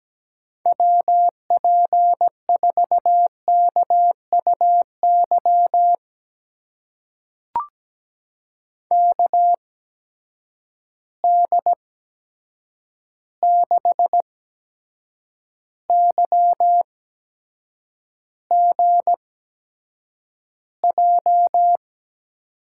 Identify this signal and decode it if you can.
Morse code, heard as a keyed tone, about 17 wpm, spaced as for 7 wpm, 700 Hz. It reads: WP4KUY KD6YGJ
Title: USA Call Signs - 8x Character Spacing - 17wpm